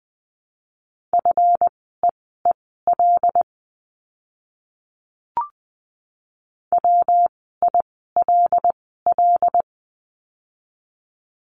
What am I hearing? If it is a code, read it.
Morse code, heard as a keyed tone, about 20 wpm, spaced as for 14 wpm, 700 Hz. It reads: FEEL WILL
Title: 244 Lesson 39f - Letter z - Words 20wpm